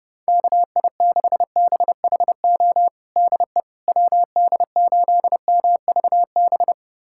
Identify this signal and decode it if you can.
Morse code, heard as a keyed tone, about 32 wpm, 700 Hz. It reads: KI6BHO DE WD8MVB
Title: USA Callsign DE Callsign 30wpm